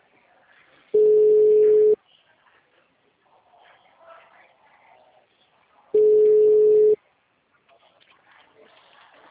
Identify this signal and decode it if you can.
Ringback tone (Europe, 425 Hz).